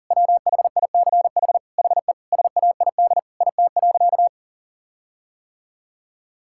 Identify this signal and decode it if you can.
Morse code, heard as a keyed tone, about 40 wpm, 700 Hz. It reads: WHICH HE SAID IT.
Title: Sentences from Top 1600 Words 40wpm